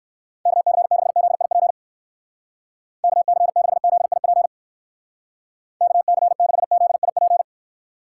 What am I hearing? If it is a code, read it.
Morse code, heard as a keyed tone, about 58 wpm, 700 Hz. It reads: KC6ZIP KC6ZIP KC6ZIP